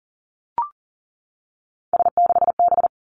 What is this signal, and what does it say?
Morse code, heard as a keyed tone, about 38 wpm, 700 Hz. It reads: S6B
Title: Three Challenging Letter-Number Combinations 40wpm